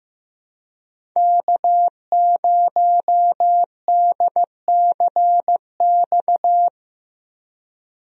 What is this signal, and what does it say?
Morse code, heard as a keyed tone, about 15 wpm, 700 Hz. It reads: K0DCX
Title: USA Call Signs - 5 Characters - Repeated 2x - 15wpm